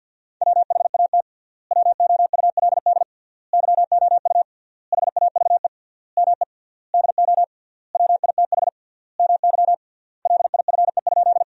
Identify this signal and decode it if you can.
Morse code, heard as a keyed tone, about 53 wpm, 700 Hz. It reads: WHAT WOULD YOU HAVE ME DO WITH MY LIFE?
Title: 3x Word-Spacing - Sentences from Top 300 Words - 50wpm